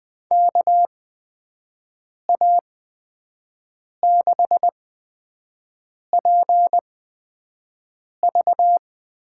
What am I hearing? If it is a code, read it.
Morse code, heard as a keyed tone, about 21 wpm, spaced as for 5 wpm, 700 Hz. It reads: KA6PV
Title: USA Call Signs - 6 Characters - 8x Character Spacing - 20wpm